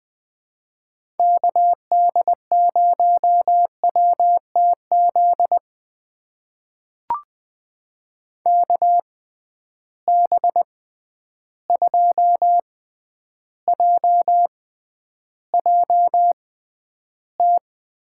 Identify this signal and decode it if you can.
Morse code, heard as a keyed tone, about 20 wpm, spaced as for 10 wpm, 700 Hz. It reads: KD0WTZ KB2JJT
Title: USA Call Signs - 6 Characters - 6x Character Spacing - 20wpm